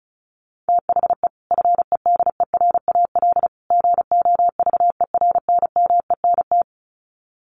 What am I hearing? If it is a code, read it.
Morse code, heard as a keyed tone, about 34 wpm, 700 Hz. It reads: THE FEDERAL GOVERNMENT
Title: Sets of 3 Words 35wpm